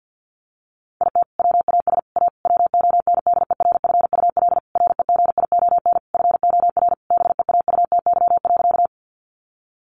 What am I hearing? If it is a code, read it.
Morse code, heard as a keyed tone, about 50 wpm, 700 Hz. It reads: IT WAS A WONDERFUL REGION FOR BEAUTY.
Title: Sentences from Top 1500 Words 50wpm